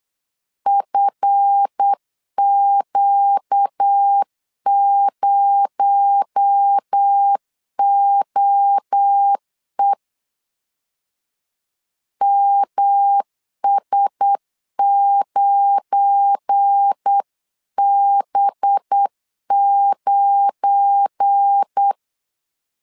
Morse code, heard as a keyed tone, about 8 wpm, 800 Hz. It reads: FQ0OE MS9B9
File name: LEZIONI / 7 GF9Z / 7 GF9 8wpm 3545